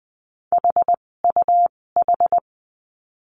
Morse code, heard as a keyed tone, about 20 wpm, spaced as for 16 wpm, 700 Hz. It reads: HUH
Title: 256 Lesson 42a - Words 12-20wpm